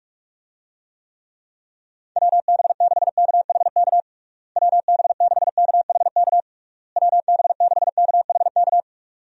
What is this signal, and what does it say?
Morse code, heard as a keyed tone, about 46 wpm, 700 Hz. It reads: WB6KHK WB6KHK WB6KHK
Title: USA Call Signs - 6 Characters - Repeated 3x - 45wpm